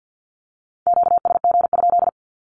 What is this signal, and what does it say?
Morse code, heard as a keyed tone, about 50 wpm, 700 Hz. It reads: QSG?
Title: Q-Codes with Spoken Meaning - 50wpm